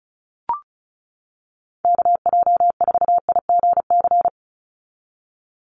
Morse code, heard as a keyed tone, about 36 wpm, 700 Hz. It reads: KJ4IGC
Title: USA Call Signs 35wpm